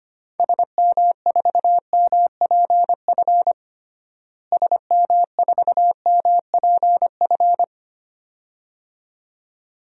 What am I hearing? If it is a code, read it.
Morse code, heard as a keyed tone, about 25 wpm, 700 Hz. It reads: SM4MPF SM4MPF
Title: DXCC-100 Call Signs - 6 Characters - Repeated 2x - 25wpm